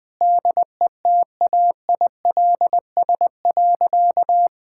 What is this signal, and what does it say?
Morse code, heard as a keyed tone, about 20 wpm, 700 Hz. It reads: DETAILS.
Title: Sentences with 9-Word Phrases - Spoken Beginning - 20wpm